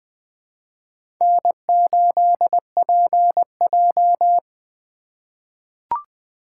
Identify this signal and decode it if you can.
Morse code, heard as a keyed tone, about 20 wpm, 700 Hz. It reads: N8PJ
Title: USA Call Signs - 4 Characters - 3x Character Spacing - 20wpm